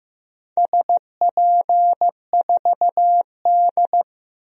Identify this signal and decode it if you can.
Morse code, heard as a keyed tone, about 15 wpm, 700 Hz. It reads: SP4D
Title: DXCC-100 Call Signs - 4 Characters - 15wpm